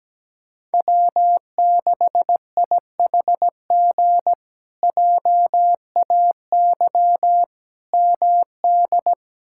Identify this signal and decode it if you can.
Morse code, heard as a keyed tone, about 17 wpm, 700 Hz. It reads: W6IHG JAY MD
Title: K1USN SST QSO Practice - 17wpm